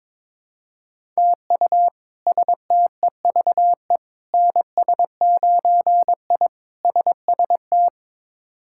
Morse code, heard as a keyed tone, about 22 wpm, 700 Hz. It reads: TU STEVE NS9I SST
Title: K1USN SST QSO Practice - 22wpm